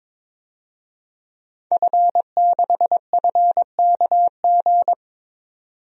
Morse code, heard as a keyed tone, about 22 wpm, 700 Hz. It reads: F6FKG
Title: DXCC-100 Call Signs - 5 Characters - 22wpm